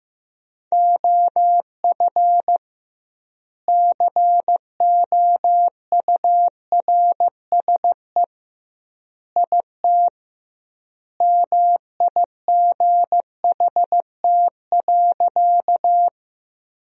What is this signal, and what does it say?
Morse code, heard as a keyed tone, about 15 wpm, 700 Hz. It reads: OF COURSE IT MIGHT.
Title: 2x Word-Spacing - Sentences from Top 500 Words - 15wpm